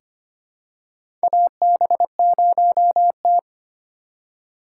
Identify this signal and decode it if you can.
Morse code, heard as a keyed tone, about 25 wpm, 700 Hz. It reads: AB0T
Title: USA Call Signs - 4 Characters - 4x Character Spacing - 25wpm